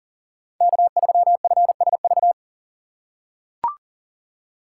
Morse code, heard as a keyed tone, about 40 wpm, 700 Hz. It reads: K3FSV
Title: USA Call Signs - 2x Character Spacing - 40wpm